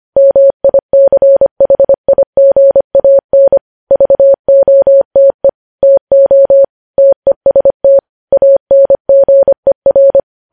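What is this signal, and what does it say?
Morse code, heard as a keyed tone, about 25 wpm, 550 Hz. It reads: MICHIGAN VOTE TO TEST ANGER
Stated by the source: News headlines sent in Morse code at 25 wpm.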